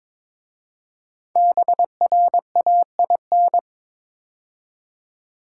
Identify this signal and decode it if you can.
Morse code, heard as a keyed tone, about 22 wpm, 700 Hz. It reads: BRAIN
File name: 5 Letter Words - First 3 Letters Given - 22wpm